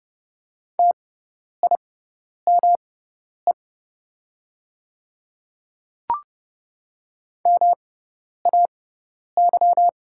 Morse code, heard as a keyed tone, about 30 wpm, spaced as for 10 wpm, 700 Hz. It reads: TIME MAY
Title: 6x Character Spacing - Top 200 Words - 30wpm